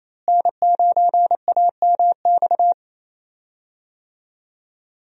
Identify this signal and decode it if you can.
Morse code, heard as a keyed tone, about 28 wpm, 700 Hz. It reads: N9AMX
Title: USA Call Signs - 5 Characters - Repeated 3x - 28wpm